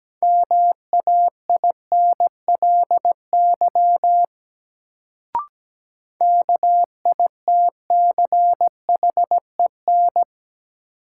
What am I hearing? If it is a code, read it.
Morse code, heard as a keyed tone, about 17 wpm, 700 Hz. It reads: MAINLY KITCHEN
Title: Sentences from Top 1400 Words - New Word Drill - 17wpm